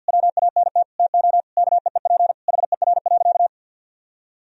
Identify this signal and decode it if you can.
Morse code, heard as a keyed tone, about 50 wpm, 700 Hz. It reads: WANT TO KEEP HER.
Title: Sentences from Top 1000 Words - Review Part 2 - 50wpm